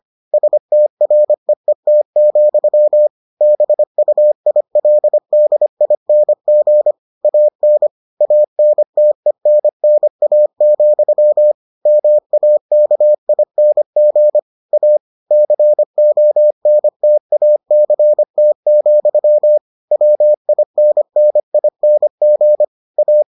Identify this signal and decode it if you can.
Morse code, heard as a keyed tone, about 25 wpm, 600 Hz. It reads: STREET, BUILDING AN ANTENNA, MAKING A CONTACT, WINNING A